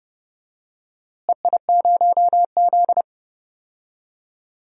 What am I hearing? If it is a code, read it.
Morse code, heard as a keyed tone, about 32 wpm, 700 Hz. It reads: EI0Z
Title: DXCC-100 Call Signs - 4 Characters - Repeated 2x - 30wpm